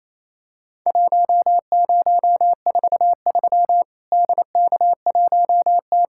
Morse code, heard as a keyed tone, about 28 wpm, 700 Hz. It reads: 1043 DK1T
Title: Worked All Europe DX Contest - QTC Exchange - 28wpm